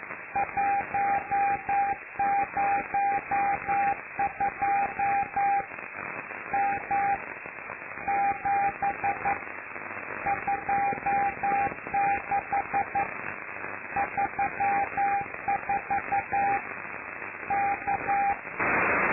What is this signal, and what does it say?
Morse code, heard as a keyed tone, about 11 wpm, 800 Hz, with radio noise. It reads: 102 M 7 26 34 K
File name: lu4aao / Globo Amsat